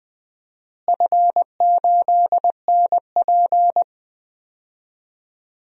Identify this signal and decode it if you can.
Morse code, heard as a keyed tone, about 20 wpm, 700 Hz. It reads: F8NP
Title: DXCC-100 Call Signs - 4 Characters - Repeated 3x - 20wpm